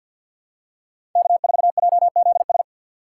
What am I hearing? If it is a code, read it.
Morse code, heard as a keyed tone, about 47 wpm, 700 Hz. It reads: K4JZS
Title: USA Call Signs - 5 Characters - 4x Character Spacing - 50wpm